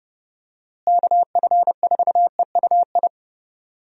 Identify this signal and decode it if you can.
Morse code, heard as a keyed tone, about 30 wpm, 700 Hz. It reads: KF4EUI
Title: USA Call Signs - 6 Characters - Repeated 2x - 30wpm